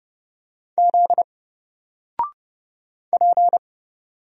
Morse code, heard as a keyed tone, about 30 wpm, 700 Hz. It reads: Z P
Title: Single Character (Letter:Number:Punctuation:Prosign) 30wpm